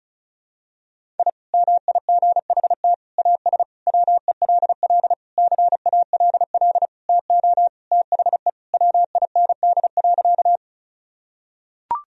Morse code, heard as a keyed tone, about 34 wpm, 700 Hz. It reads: I MIGHT AS WELL CALL TO THE WIND.
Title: Sentences from Top 1000 Words - Review Part 2 - 35wpm